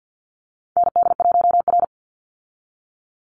Morse code, heard as a keyed tone, about 50 wpm, 700 Hz. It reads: ND1R
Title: Call Signs - 50wpm